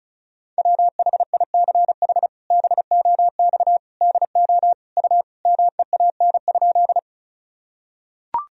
Morse code, heard as a keyed tone, about 36 wpm, 700 Hz. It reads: WHICH BOX DO U MEAN?
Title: Sentences from Top 500 Words and Abbreviations 35wpm